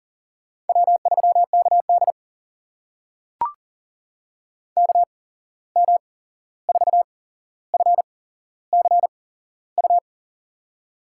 Morse code, heard as a keyed tone, about 37 wpm, spaced as for 15 wpm, 700 Hz. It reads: W3KD KM4FCU
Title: USA Call Signs - 8x Character Spacing - 40wpm